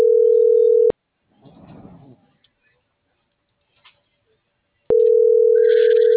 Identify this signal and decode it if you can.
Ringback tone (North America), 440 + 480 Hz.